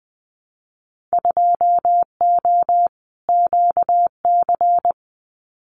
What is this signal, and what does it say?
Morse code, heard as a keyed tone, about 20 wpm, 700 Hz. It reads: 2O QC
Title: Winter Field Day Exchanges 20wpm